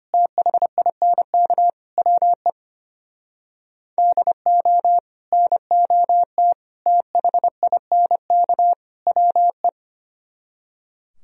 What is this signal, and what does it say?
Morse code, heard as a keyed tone, about 26 wpm, 700 Hz. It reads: THINK WE DO NOT THINK WE
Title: Speed-Racing - Sets of 4 Words 25wpm